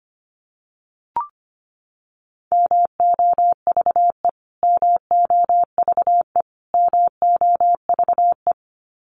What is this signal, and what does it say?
Morse code, heard as a keyed tone, about 25 wpm, 700 Hz. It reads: MOVE MOVE MOVE
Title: Top 200 Words Repeated 25wpm